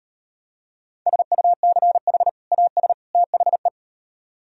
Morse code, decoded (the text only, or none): SUCH AS THE